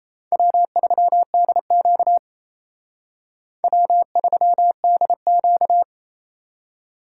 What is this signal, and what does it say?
Morse code, heard as a keyed tone, about 35 wpm, 700 Hz. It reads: W3DQ W3DQ
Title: Speed Racing - USA 1x2 Call Signs 28wpm